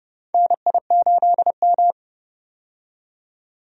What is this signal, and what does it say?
Morse code, heard as a keyed tone, about 30 wpm, 700 Hz. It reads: NI8M
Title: USA Call Signs - 4 Characters - 30wpm